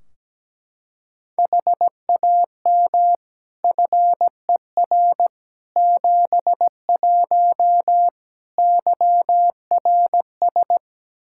Morse code, heard as a keyed tone, about 17 wpm, 700 Hz. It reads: HAM FER 71 YRS
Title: K1USN SST QSO Practice 17wpm